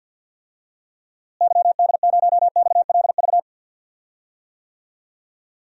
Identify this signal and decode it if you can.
Morse code, heard as a keyed tone, about 52 wpm, 700 Hz. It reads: YD0XLV
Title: DXCC-100 Call Signs - 6 Characters - Repeated 3x - 50wpm